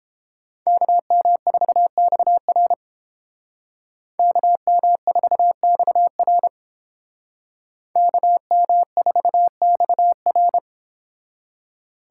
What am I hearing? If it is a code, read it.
Morse code, heard as a keyed tone, about 31 wpm, 700 Hz. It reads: KM4XR KM4XR KM4XR